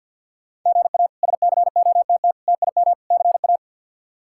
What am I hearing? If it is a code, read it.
Morse code, heard as a keyed tone, about 53 wpm, 700 Hz. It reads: GA SCOTT TIM CA
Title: K1USN SST QSO Practice - 50wpm